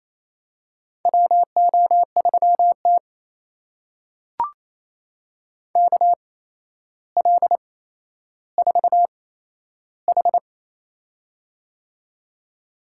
Morse code, heard as a keyed tone, about 28 wpm, spaced as for 12 wpm, 700 Hz. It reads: WO3T KL4H
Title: USA Call Signs - 4 Characters - 8x Character Spacing - 28wpm